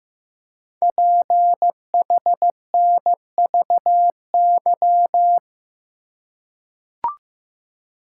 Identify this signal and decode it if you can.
Morse code, heard as a keyed tone, about 15 wpm, 700 Hz. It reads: PHNVY
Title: Code Groups - 5 Letters - 15wpm